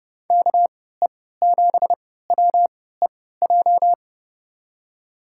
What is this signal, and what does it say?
Morse code, heard as a keyed tone, about 30 wpm, spaced as for 17 wpm, 700 Hz. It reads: KE7WEJ
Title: USA Call Signs - 6 Characters - 3x Character Spacing - 30wpm